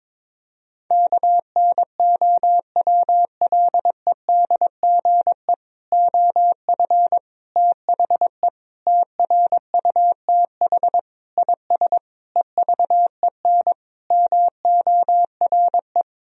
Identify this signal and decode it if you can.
Morse code, heard as a keyed tone, about 22 wpm, 700 Hz. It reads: KNOWLEDGE OF THE TRUTH IS EVEN MORE
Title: Sentences from Top 1000 Words - Review Part 3 - 22wpm